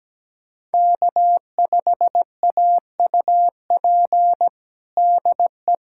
Morse code, heard as a keyed tone, about 17 wpm, 700 Hz. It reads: K5AUP DE
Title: USA Callsign DE Callsign 17wpm